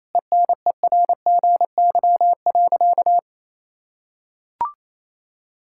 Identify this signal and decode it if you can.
Morse code, heard as a keyed tone, about 28 wpm, 700 Hz. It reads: ENERGY.
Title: Sentences from Top 1000 Words - Review Part 6 - 28wpm